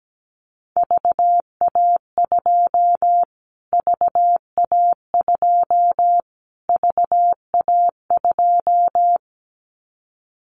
Morse code, heard as a keyed tone, about 17 wpm, 700 Hz. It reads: VA2 VA2 VA2
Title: Callsign Prefixes for Top 25 DXCC - Repeated 3x - 17wpm